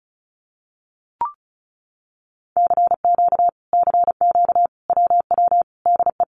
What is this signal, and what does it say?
Morse code, heard as a keyed tone, about 34 wpm, 700 Hz. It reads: CQ CQ WW DE
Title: Common QSO Phrases Encore 35wpm